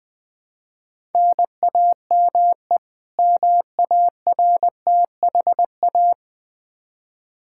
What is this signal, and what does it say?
Morse code, heard as a keyed tone, about 20 wpm, 700 Hz. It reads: NAME MARTHA
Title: Practice QSO - Exchange 1 - Beginning - 20wpm